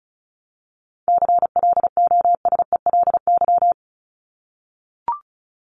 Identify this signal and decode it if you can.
Morse code, heard as a keyed tone, about 34 wpm, 700 Hz. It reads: CLOSELY
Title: Top 1251-1500 Words 35wpm